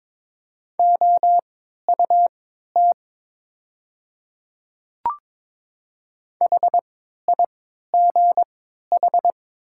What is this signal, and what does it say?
Morse code, heard as a keyed tone, about 22 wpm, spaced as for 12 wpm, 700 Hz. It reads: OUT HIGH
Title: 3x Character Spacing - Top 300 Words - 22wpm